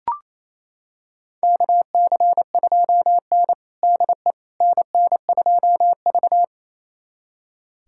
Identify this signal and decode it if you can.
Morse code, heard as a keyed tone, about 28 wpm, 700 Hz. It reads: KC2N DE NN2V
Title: USA Callsign DE Callsign - 4 Characters - 28wpm